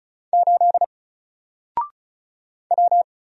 Morse code, heard as a keyed tone, about 36 wpm, 700 Hz. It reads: 8 W
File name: Single-Letter-Number-35wpm